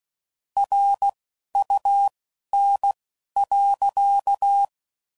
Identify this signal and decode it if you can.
Morse code, heard as a keyed tone, about 16 wpm, spaced as for 11 wpm, 800 Hz. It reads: RUN.
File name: CW Code Groups @10WPM / QSOs / QSO kc7sac